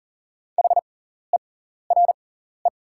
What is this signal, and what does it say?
Morse code, heard as a keyed tone, about 40 wpm, spaced as for 14 wpm, 700 Hz. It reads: HERE
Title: 6x Character Spacing - Top 200 Words - 40wpm